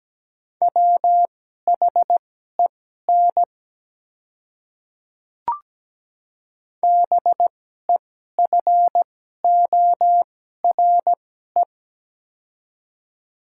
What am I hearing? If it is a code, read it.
Morse code, heard as a keyed tone, about 17 wpm, spaced as for 12 wpm, 700 Hz. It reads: WHEN BEFORE
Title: 2x Character Spacing - Top 200 Words - 17wpm